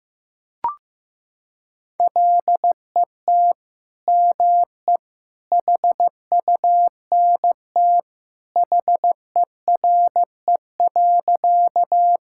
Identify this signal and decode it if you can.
Morse code, heard as a keyed tone, about 15 wpm, 700 Hz. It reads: LET ME HUNT HERE.
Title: Sentences from Top 1600 Words 15wpm